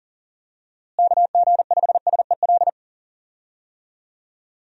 Morse code, heard as a keyed tone, about 40 wpm, 700 Hz. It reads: KG5SEL
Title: USA Call Signs - 6 Characters - Repeated 3x - 40wpm